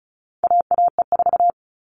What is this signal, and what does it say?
Morse code, heard as a keyed tone, about 34 wpm, 700 Hz. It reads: AAE4
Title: All Cut Numbers - 4-Digit 35wpm